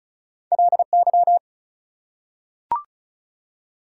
Morse code, heard as a keyed tone, about 34 wpm, 700 Hz. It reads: LY
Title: Prefixes and Suffixes 35wpm